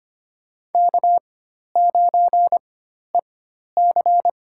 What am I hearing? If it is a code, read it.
Morse code, heard as a keyed tone, about 25 wpm, spaced as for 12 wpm, 700 Hz. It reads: K9EC
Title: USA Call Signs - 4 Characters - 4x Character Spacing - 25wpm